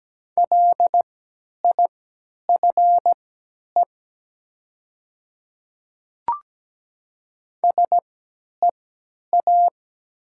Morse code, heard as a keyed tone, about 17 wpm, spaced as for 10 wpm, 700 Hz. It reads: LIFE SEA
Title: 3x Character Spacing - Top 500 Words - 17wpm